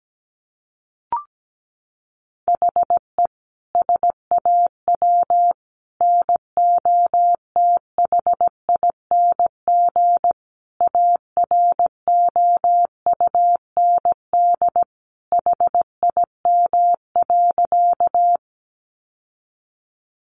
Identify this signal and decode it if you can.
Morse code, heard as a keyed tone, about 17 wpm, 700 Hz. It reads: HE SAW NOTHING AROUND HIM.
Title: Sentences from Top 600 Words 17wpm